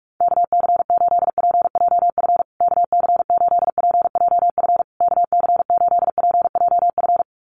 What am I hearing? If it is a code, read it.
Morse code, heard as a keyed tone, about 48 wpm, 700 Hz. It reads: KC8PJF KC8PJF KC8PJF
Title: USA Call Signs Repeated 3x 45wpm